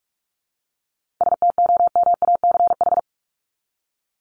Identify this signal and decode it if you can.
Morse code, heard as a keyed tone, about 44 wpm, 700 Hz. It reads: STOMACH
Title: Top 1501-1750 Words 45wpm